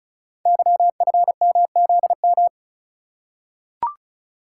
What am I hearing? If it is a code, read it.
Morse code, heard as a keyed tone, about 36 wpm, 690 Hz. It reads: YFMZM
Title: Code Groups - 5 Letters - 35wpm